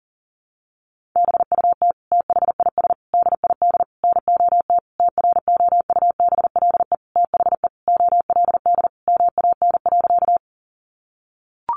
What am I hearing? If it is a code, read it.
Morse code, heard as a keyed tone, about 40 wpm, 700 Hz. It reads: BUT THIS DID NOT TROUBLE THE OLD MAN.
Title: Sentences from Top 1200 Words 40wpm